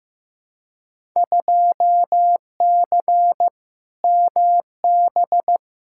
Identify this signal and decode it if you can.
Morse code, heard as a keyed tone, about 15 wpm, 700 Hz. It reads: 2C MB
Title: Field Day Exchanges Encore - 15wpm